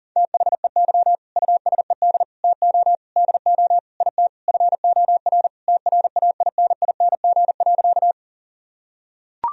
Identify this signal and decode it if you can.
Morse code, heard as a keyed tone, about 40 wpm, 700 Hz. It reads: THEY USED TO DO IT FOR TRAINING.
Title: Sentences from Top 1000 Words - Review Part 4 - 40wpm